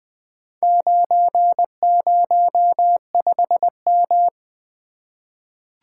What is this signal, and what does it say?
Morse code, heard as a keyed tone, about 20 wpm, 700 Hz. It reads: 905M
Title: Four Letter-Number Combinations - 20wpm